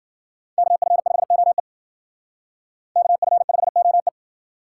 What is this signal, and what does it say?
Morse code, heard as a keyed tone, about 56 wpm, 700 Hz. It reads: KF5OE KF5OE